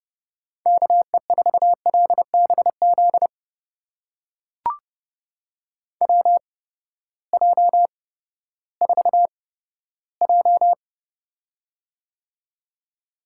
Morse code, heard as a keyed tone, about 30 wpm, spaced as for 15 wpm, 700 Hz. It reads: KE4LBZ WJ4J
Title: USA Call Signs - 8x Character Spacing - 30wpm